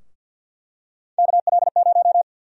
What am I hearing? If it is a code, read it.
Morse code, heard as a keyed tone, about 48 wpm, 700 Hz. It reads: KL0
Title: USA Call Sign Prefixes - Repeated 2x - 50wpm